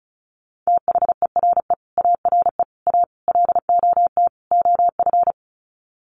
Morse code, heard as a keyed tone, about 34 wpm, 700 Hz. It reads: THERE ARE A LOT OF
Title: Sets of 5 Words 35wpm